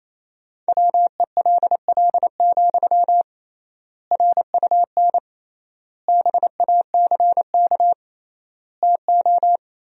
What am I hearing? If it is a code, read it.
Morse code, heard as a keyed tone, about 28 wpm, 700 Hz. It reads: WELL, RUN BACK TO
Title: 3x Word-Spacing - Sentences from Top 300 Words - 28wpm